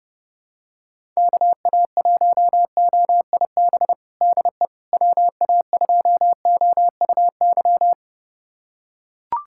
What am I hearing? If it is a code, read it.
Morse code, heard as a keyed tone, about 32 wpm, 700 Hz. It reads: KA1OIB DE WA2OUY
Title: USA Callsign DE Callsign 30wpm